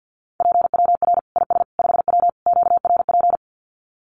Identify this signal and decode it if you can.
Morse code, heard as a keyed tone, about 50 wpm, 700 Hz. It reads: PWR IS 5W QRP
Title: Common QSO Phrases 50wpm